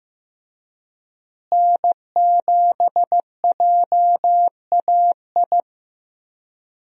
Morse code, heard as a keyed tone, about 15 wpm, 700 Hz. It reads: N7JAI
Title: USA Call Signs - 5 Characters - Repeated 3x - 15wpm